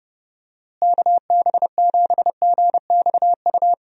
Morse code, heard as a keyed tone, about 29 wpm, 700 Hz. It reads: KB7GXU
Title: USA Call Signs - 6 Characters - Repeated 3x - 30wpm